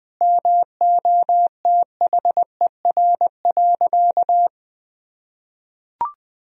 Morse code, heard as a keyed tone, about 20 wpm, 700 Hz. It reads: MOTHER.
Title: Sentences from Top 1000 Words - Review Part 4 - 20wpm